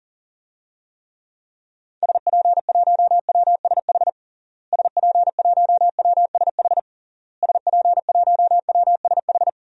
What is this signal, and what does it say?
Morse code, heard as a keyed tone, about 37 wpm, 700 Hz. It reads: SP1WSH SP1WSH SP1WSH
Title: DXCC-100 Call Signs - 6 Characters - Repeated 3x - 40wpm